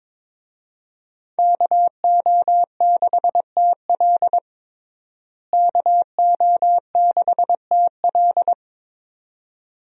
Morse code, heard as a keyed tone, about 21 wpm, 700 Hz. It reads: KO6TL KO6TL
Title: USA Call Signs - 5 Characters - Repeated 2x - 22wpm